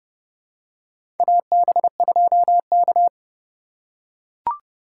Morse code, heard as a keyed tone, about 32 wpm, 700 Hz. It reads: AB2K
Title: USA Call Signs - 4 Characters - 4x Character Spacing - 30wpm